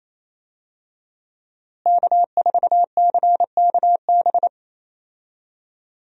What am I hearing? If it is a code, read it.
Morse code, heard as a keyed tone, about 28 wpm, 700 Hz. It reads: K4CKB
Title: USA Call Signs - 5 Characters - Repeated 3x - 28wpm